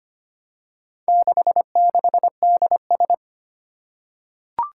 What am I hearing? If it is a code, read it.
Morse code, heard as a keyed tone, about 25 wpm, 700 Hz. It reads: 66DS
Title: Four Challenging Letter-Number Combinations 25wpm